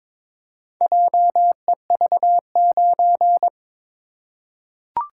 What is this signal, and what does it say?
Morse code, heard as a keyed tone, about 22 wpm, 700 Hz. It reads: JEV9
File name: practice / 22wpm / Four Letter-Number Combinations - 22wpm